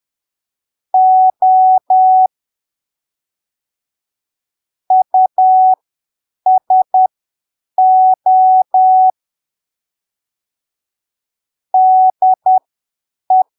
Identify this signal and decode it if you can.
Morse code, heard as a keyed tone, about 10 wpm, spaced as for 7 wpm, 750 Hz. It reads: O USO DE
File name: arquivos / texto / texto 03 10ppm